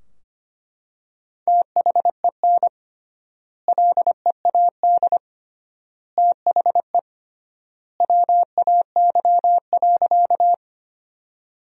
Morse code, heard as a keyed tone, about 25 wpm, 700 Hz. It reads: THEN LEAD THE WAY.
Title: 3x Word-Spacing - Sentences from Top 500 Words - 25wpm